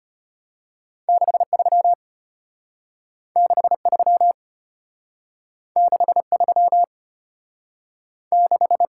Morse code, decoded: 63 63 63 6

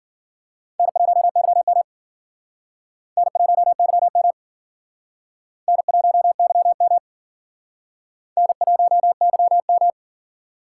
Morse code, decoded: N1YM N1YM N1YM N1YM

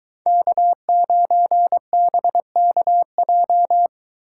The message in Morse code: K9BKJ

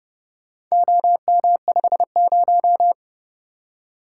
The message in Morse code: OM50

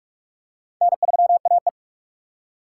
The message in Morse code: N3AE

